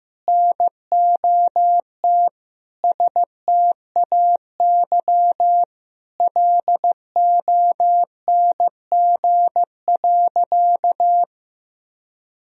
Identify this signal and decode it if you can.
Morse code, heard as a keyed tone, about 15 wpm, 700 Hz. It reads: NOT STAY LONG.
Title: Sentences from Top 1000 Words - Review Part 3 - 15wpm